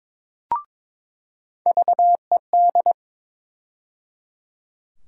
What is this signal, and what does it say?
Morse code, heard as a keyed tone, about 22 wpm, 700 Hz. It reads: VED
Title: USA Call Sign Postfixes - 3 Letters 22wpm